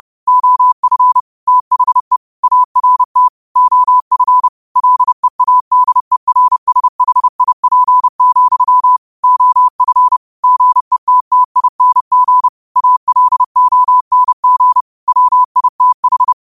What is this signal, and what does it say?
Morse code, heard as a keyed tone, about 30 wpm, 1000 Hz. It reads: OF THE ART OF LEADERSHIP, OF GETTING ALONG WITH
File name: qotdfiles / 30wpm